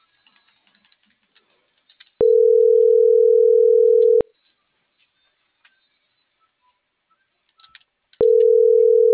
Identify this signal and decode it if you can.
Ringback tone (North America), 440 + 480 Hz.